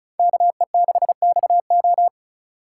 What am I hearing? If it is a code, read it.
Morse code, heard as a keyed tone, about 46 wpm, 700 Hz. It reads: KE6XO